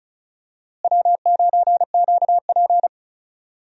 Morse code, heard as a keyed tone, about 36 wpm, 700 Hz. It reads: W9QP
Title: USA Call Signs - 4 Characters - Repeated 3x - 35wpm